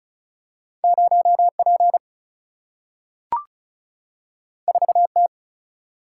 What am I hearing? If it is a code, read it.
Morse code, heard as a keyed tone, about 36 wpm, 700 Hz. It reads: 0P 4T
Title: Single-Letter Single-Number 35wpm